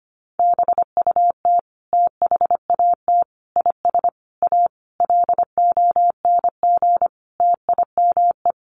Morse code, heard as a keyed tone, about 25 wpm, 700 Hz. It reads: BUT THAT IS A LONG TIME
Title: Sentences from Top 600 Words 25wpm